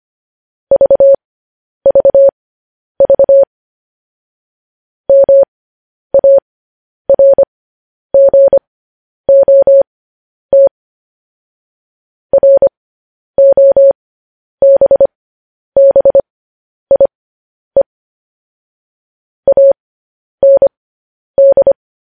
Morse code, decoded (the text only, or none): VVV MARGOT ROBBIE AND